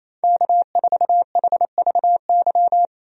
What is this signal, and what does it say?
Morse code, heard as a keyed tone, about 28 wpm, 700 Hz. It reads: K4HVY
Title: USA Call Signs - 5 Characters - 28wpm